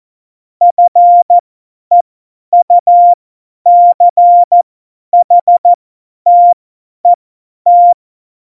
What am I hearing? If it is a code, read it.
Morse code, heard as a keyed tone, about 14 wpm, spaced as for 10 wpm, 700 Hz. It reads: FEUCHTET
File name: Morsekurs / Witze / Witz 17. November 2020, 50 F70